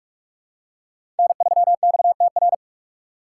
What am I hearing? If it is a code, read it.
Morse code, heard as a keyed tone, about 46 wpm, 700 Hz. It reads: N3XTR
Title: USA Call Signs - 5 Characters - 3x Character Spacing - 45wpm